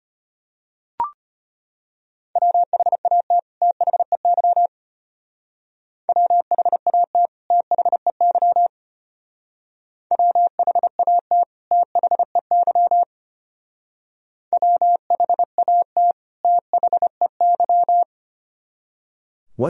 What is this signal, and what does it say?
Morse code, heard as a keyed tone, about 31 wpm, 700 Hz. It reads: WHAT THEY WHAT THEY WHAT THEY WHAT THEY